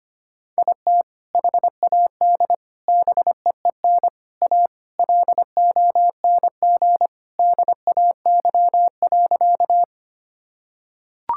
Morse code, heard as a keyed tone, about 25 wpm, 700 Hz. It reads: IT HAD BEEN A LONG DAY.
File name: practice / 25wpm / Sentences with 2-Word Phrases - 25wpm